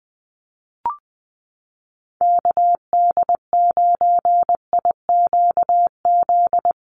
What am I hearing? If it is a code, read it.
Morse code, heard as a keyed tone, about 20 wpm, 700 Hz. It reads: KD9IQZ
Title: USA Call Signs Repeated 2x 20wpm